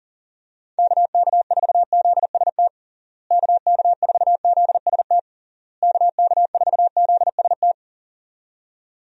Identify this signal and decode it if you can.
Morse code, heard as a keyed tone, about 40 wpm, 700 Hz. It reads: KK4ZST KK4ZST KK4ZST
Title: USA Call Signs - 6 Characters - Repeated 3x - 40wpm